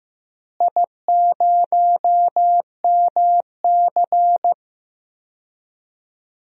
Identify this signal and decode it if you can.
Morse code, heard as a keyed tone, about 15 wpm, 700 Hz. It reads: I0MC
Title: DXCC-100 Call Signs - 4 Characters - Repeated 3x - 15wpm